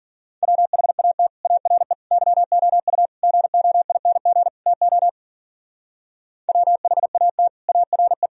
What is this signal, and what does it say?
Morse code, heard as a keyed tone, about 49 wpm, 700 Hz. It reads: WHAT ARE YOU GOING TO WHAT ARE
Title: Speed-Racing - Sets of 5 Words 40wpm